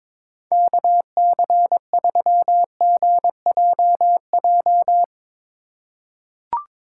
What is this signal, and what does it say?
Morse code, heard as a keyed tone, about 22 wpm, 700 Hz. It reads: KC3GJJ
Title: USA Call Signs - 6 Characters - 3x Character Spacing - 22wpm